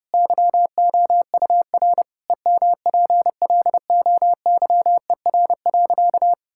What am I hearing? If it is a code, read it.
Morse code, heard as a keyed tone, about 30 wpm, 700 Hz. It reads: YOUR EMPLOYER.
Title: Sentences from Top 1600 Words 30wpm